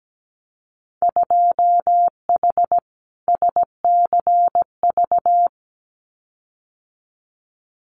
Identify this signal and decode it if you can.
Morse code, heard as a keyed tone, about 17 wpm, 700 Hz. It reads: 2H SCV
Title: Winter Field Day Exchanges Encore 17wpm